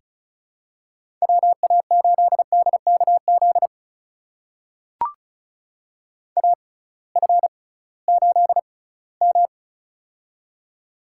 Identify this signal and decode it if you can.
Morse code, heard as a keyed tone, about 36 wpm, spaced as for 20 wpm, 700 Hz. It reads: WA8DKZ AF8M
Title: USA Call Signs - 6x Character Spacing - 35wpm